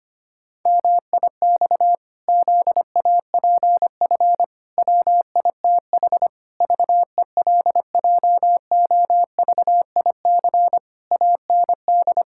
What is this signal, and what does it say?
Morse code, heard as a keyed tone, about 25 wpm, 700 Hz. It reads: MIX ZAPF WITH VELJOVIC AND
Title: Pangram Challenge v2 - 25wpm